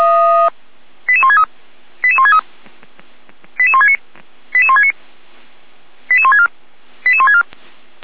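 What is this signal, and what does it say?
Five-tone selective call (ZVEI), heard as quick runs of five tones, tones of about 70 ms. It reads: 88163 88163 88169 88169 88164 88164
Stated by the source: ZVEI five-tone selective call received on SDR, from feuerwehr-orlamuende.org (fire brigade Orlamünde).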